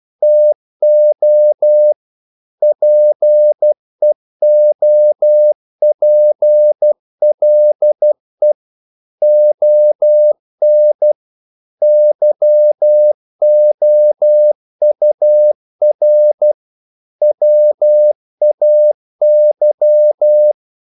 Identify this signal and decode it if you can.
Morse code, heard as a keyed tone, about 12 wpm, 600 Hz. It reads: TO PEOPLE ON YOUR WAY
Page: English quotations in CW (telegraphy - morse)